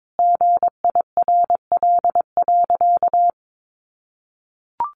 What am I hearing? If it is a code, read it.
Morse code, heard as a keyed tone, about 22 wpm, 700 Hz. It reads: GIRL.
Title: Additional Easy Phrases 22wpm